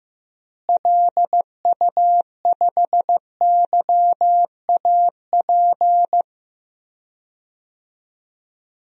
Morse code, heard as a keyed tone, about 15 wpm, 700 Hz. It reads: LU5YAP
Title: DXCC-100 Call Signs - 6 Characters - Repeated 3x - 15wpm